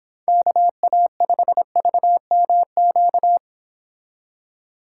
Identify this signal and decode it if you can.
Morse code, heard as a keyed tone, about 25 wpm, 700 Hz. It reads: KA5VMQ
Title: Speed Racing - USA 2x3 Call Signs 22wpm